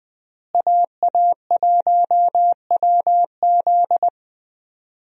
Morse code, heard as a keyed tone, about 20 wpm, 700 Hz. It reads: AA1WZ
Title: USA Call Signs - 5 Characters - Repeated 2x - 20wpm